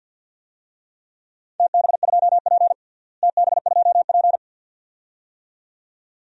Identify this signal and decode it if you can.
Morse code, heard as a keyed tone, about 51 wpm, 700 Hz. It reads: TB2P TB2P
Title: DXCC-100 Call Signs - 4 Characters - Repeated 2x - 50wpm